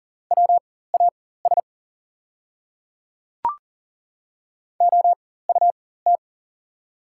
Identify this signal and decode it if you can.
Morse code, heard as a keyed tone, about 40 wpm, spaced as for 19 wpm, 690 Hz. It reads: WAS OUT
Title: 4x Character Spacing - Top 100 Words - 40wpm